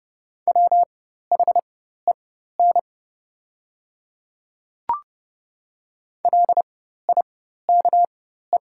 Morse code, heard as a keyed tone, about 30 wpm, spaced as for 14 wpm, 700 Hz. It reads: WHEN LIKE
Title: 4x Character Spacing - Top 100 Words - 30wpm